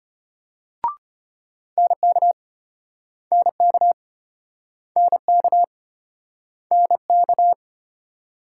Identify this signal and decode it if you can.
Morse code, heard as a keyed tone, about 32 wpm, 700 Hz. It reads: NK NK NK NK